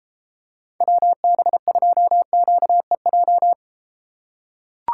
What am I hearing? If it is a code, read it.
Morse code, heard as a keyed tone, about 33 wpm, 700 Hz. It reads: WB2QEJ